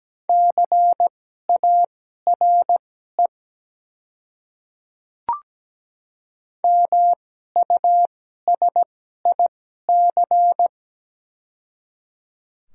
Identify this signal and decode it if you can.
Morse code, heard as a keyed tone, about 17 wpm, spaced as for 12 wpm, 700 Hz. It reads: CARE MUSIC
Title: 2x Character Spacing - Top 300 Words - 17wpm